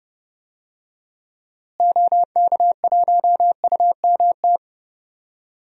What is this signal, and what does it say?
Morse code, heard as a keyed tone, about 30 wpm, 700 Hz. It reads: OK1UMT
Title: DXCC-100 Call Signs - 6 Characters - Repeated 3x - 30wpm